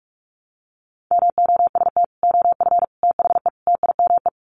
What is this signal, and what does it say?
Morse code, heard as a keyed tone, about 44 wpm, 700 Hz. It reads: MOST OF THE TIME
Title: Sets of 4 Words 45wpm